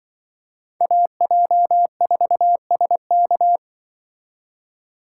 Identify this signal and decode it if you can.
Morse code, heard as a keyed tone, about 24 wpm, 700 Hz. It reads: AJ4SK
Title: Speed Racing - USA Call Signs 20wpm